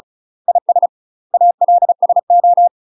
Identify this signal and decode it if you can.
Morse code, heard as a keyed tone, about 34 wpm, 700 Hz. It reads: IS ALSO
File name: UK MASTER QSOs TRAINING / 35WPM / UK MASTER QSOs 0040